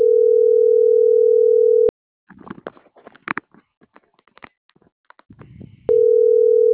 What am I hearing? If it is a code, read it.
Ringback tone (North America), 440 + 480 Hz.